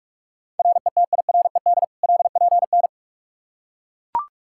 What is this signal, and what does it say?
Morse code, heard as a keyed tone, about 41 wpm, 700 Hz. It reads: RETIRED LPN
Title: Practice QSO - Exchange 3 - More Info - 45wpm